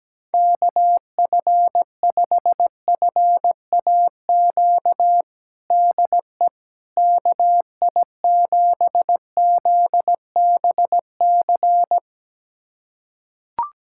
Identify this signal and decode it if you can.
Morse code, heard as a keyed tone, about 17 wpm, 700 Hz. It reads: KF5FAQ DE KI7ZBC
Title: USA Callsign DE Callsign 17wpm